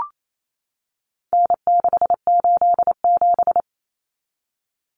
Morse code, heard as a keyed tone, about 28 wpm, 700 Hz. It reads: N687
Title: Common Cut Numbers - 4-Digit 28wpm